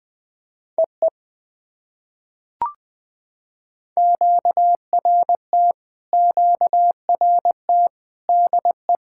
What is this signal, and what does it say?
Morse code, heard as a keyed tone, about 20 wpm, 700 Hz. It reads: EE QRT QRT DE
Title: POTA QSO Practice - 20wpm